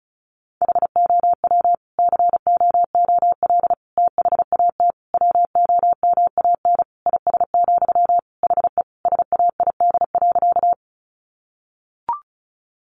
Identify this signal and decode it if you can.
Morse code, heard as a keyed tone, about 34 wpm, 700 Hz. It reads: HOW COOL THAT WOMAN IS, HE SAID.
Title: Sentences from Top 1200 Words 35wpm